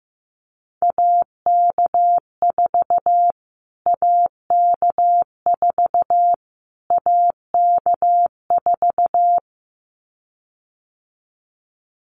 Morse code, heard as a keyed tone, about 15 wpm, 700 Hz. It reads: AK4 AK4 AK4
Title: Callsign Prefixes for Top 25 DXCC - Repeated 3x - 15wpm